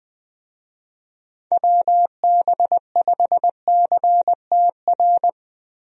Morse code, heard as a keyed tone, about 20 wpm, 700 Hz. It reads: WB5CTR
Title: USA Call Signs - 6 Characters - Repeated 2x - 20wpm